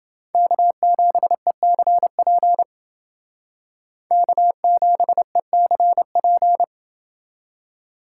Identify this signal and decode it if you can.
Morse code, heard as a keyed tone, about 29 wpm, 700 Hz. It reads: K7ECP K7ECP